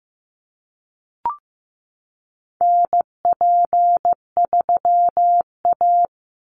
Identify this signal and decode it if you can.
Morse code, heard as a keyed tone, about 15 wpm, 700 Hz. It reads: NP3A
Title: Call Signs Repeated 3x - 15wpm